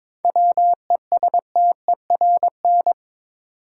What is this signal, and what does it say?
Morse code, heard as a keyed tone, about 22 wpm, 700 Hz. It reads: WESTERN
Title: Sentences from Top 1000 Words - Review Part 7 - New Word Drill - 22wpm